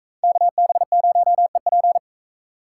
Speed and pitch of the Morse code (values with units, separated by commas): 44 wpm, 700 Hz